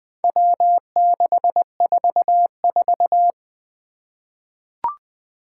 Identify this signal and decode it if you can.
Morse code, heard as a keyed tone, about 20 wpm, 700 Hz. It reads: W644
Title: Four Challenging Letter-Number Combinations 20wpm